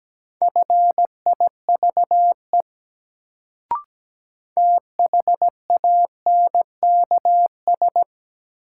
Morse code, heard as a keyed tone, about 17 wpm, 700 Hz. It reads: FIVE THANKS
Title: Sets of 5 Words - New Word Drill - 17wpm